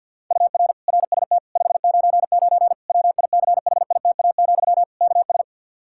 Morse code, heard as a keyed tone, about 52 wpm, 700 Hz. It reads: UR RST 599 WICHITA, KS
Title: Practice QSO - Exchange 1 - Beginning - 50wpm